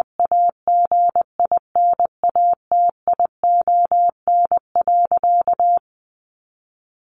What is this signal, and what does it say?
Morse code, heard as a keyed tone, about 20 wpm, 700 Hz. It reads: AGINATION.
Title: Sentences from Top 1300 Words 20wpm